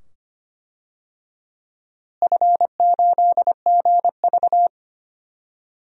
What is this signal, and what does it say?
Morse code, heard as a keyed tone, about 25 wpm, 700 Hz. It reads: F8GV
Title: DXCC-100 Call Signs - 4 Characters - Repeated 2x - 25wpm